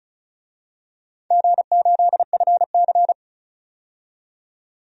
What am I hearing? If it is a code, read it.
Morse code, heard as a keyed tone, about 34 wpm, 700 Hz. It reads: G8FC
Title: DXCC-100 Call Signs - 4 Characters - 35wpm